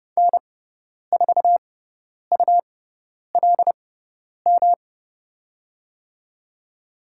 Morse code, heard as a keyed tone, about 30 wpm, spaced as for 10 wpm, 690 Hz. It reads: N4ULM
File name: Lesson 18g - Letter m - Call Signs 30wpm